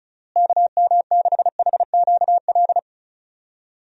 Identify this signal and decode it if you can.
Morse code, heard as a keyed tone, about 36 wpm, 700 Hz. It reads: KM6HQL
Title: USA Call Signs - 6 Characters - Repeated 3x - 35wpm